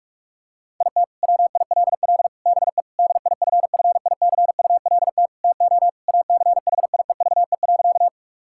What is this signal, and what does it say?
Morse code, heard as a keyed tone, about 46 wpm, 700 Hz. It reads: IT WILL BE DIFFICULT TO ACHIEVE.
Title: Sentences from Top 1000 Words - Review Part 6 - 45wpm